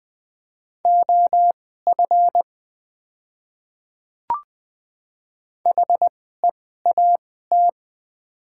Morse code, heard as a keyed tone, about 20 wpm, spaced as for 14 wpm, 690 Hz. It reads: OF HEAT
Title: 2x Character Spacing - Top 500 Words - 20wpm